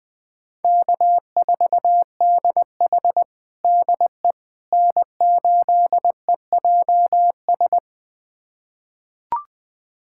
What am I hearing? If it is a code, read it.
Morse code, heard as a keyed tone, about 21 wpm, 700 Hz. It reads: K4DH DE N8EJS
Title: USA Callsign DE Callsign 20wpm